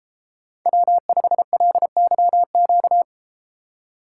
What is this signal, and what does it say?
Morse code, heard as a keyed tone, about 33 wpm, 700 Hz. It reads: W5LYQ